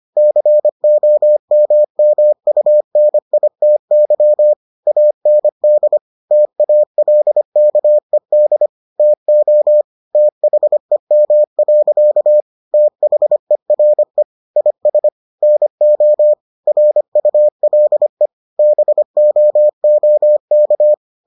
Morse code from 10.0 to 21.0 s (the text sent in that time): THEM. THERE IS NO RULE BOOK